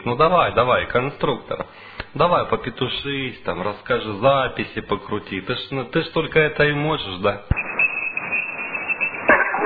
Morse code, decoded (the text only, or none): none